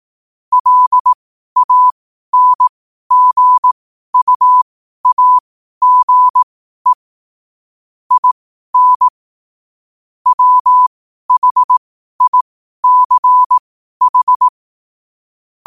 Morse code, heard as a keyed tone, about 18 wpm, spaced as for 12 wpm, 1000 Hz. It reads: LANGUAGE IN WHICH
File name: qotdfiles / 12wpm